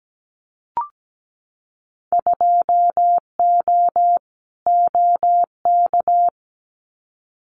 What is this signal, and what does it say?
Morse code, heard as a keyed tone, about 17 wpm, 700 Hz. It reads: 2O OK
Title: Winter Field Day Exchanges 17wpm